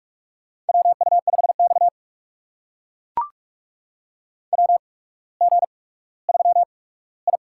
Morse code, heard as a keyed tone, about 44 wpm, spaced as for 19 wpm, 690 Hz. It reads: WU5X WG3I
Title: USA Call Signs - 4 Characters - 8x Character Spacing - 45wpm